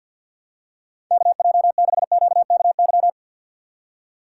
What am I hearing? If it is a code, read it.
Morse code, heard as a keyed tone, about 51 wpm, 700 Hz. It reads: KJ6QKY
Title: USA Call Signs - 6 Characters - 3x Character Spacing - 50wpm